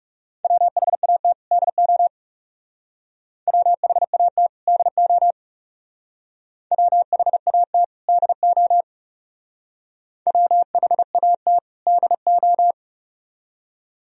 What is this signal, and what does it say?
Morse code, heard as a keyed tone, about 38 wpm, 700 Hz. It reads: WHAT DO WHAT DO WHAT DO WHAT DO